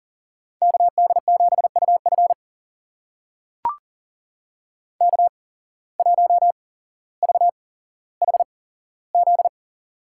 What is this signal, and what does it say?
Morse code, heard as a keyed tone, about 40 wpm, spaced as for 17 wpm, 700 Hz. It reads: KD7UF K1VHZ
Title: USA Call Signs - 5 Characters - 8x Character Spacing - 40wpm